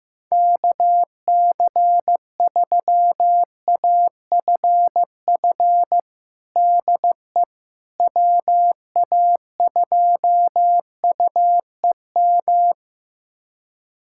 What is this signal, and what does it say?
Morse code, heard as a keyed tone, about 16 wpm, 700 Hz. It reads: KC3AFF DE WA2UEM
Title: USA Callsign DE Callsign - 6 Characters - 15wpm